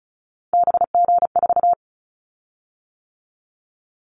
Morse code, heard as a keyed tone, about 36 wpm, 700 Hz. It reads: BG4
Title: Callsign Prefixes for Top 100 DXCC - 35wpm